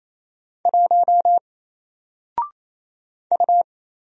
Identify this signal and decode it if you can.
Morse code, heard as a keyed tone, about 27 wpm, 700 Hz. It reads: 1 U
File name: Single-Letter-Number-28wpm